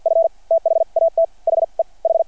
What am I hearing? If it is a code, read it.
Morse code, heard as a keyed tone, about 53 wpm, spaced as for 42 wpm, 690 Hz. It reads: F THAT HE H
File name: archived / CRSnet